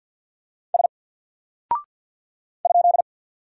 Morse code, heard as a keyed tone, about 50 wpm, 700 Hz. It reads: S ?
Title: Single Character (Letter:Number:Punctuation:Prosign) 50wpm